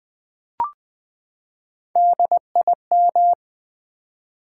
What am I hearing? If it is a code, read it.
Morse code, heard as a keyed tone, about 20 wpm, 700 Hz. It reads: DIM
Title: Build A Word - 3 Letters At A Time - 8-Letter Words - 20wpm